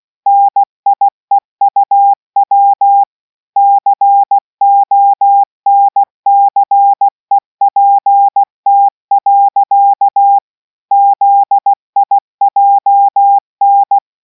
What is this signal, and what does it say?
Morse code, heard as a keyed tone, about 16 wpm, 800 Hz. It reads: NIEUW CONCEPT. ZIJN